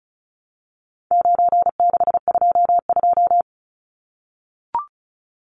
Morse code, heard as a keyed tone, about 34 wpm, 700 Hz. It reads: 9622
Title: Numbers - 4-Digit 35wpm